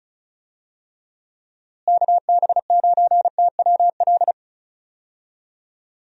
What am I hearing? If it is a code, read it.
Morse code, heard as a keyed tone, about 36 wpm, 700 Hz. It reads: KB9TWL
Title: USA Call Signs - 6 Characters - Repeated 2x - 35wpm